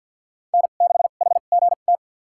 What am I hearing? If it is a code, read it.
Morse code, heard as a keyed tone, about 53 wpm, spaced as for 37 wpm, 700 Hz. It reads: N6HGT
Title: USA Call Signs - 5 Characters - 2x Character Spacing - 50wpm